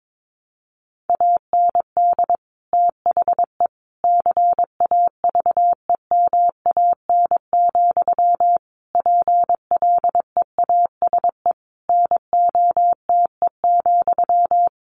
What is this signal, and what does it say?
Morse code, heard as a keyed tone, about 22 wpm, 700 Hz. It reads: AND THE CAVEMAN, PLEASE NOTE,
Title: A Sound Of Thunder 22wpm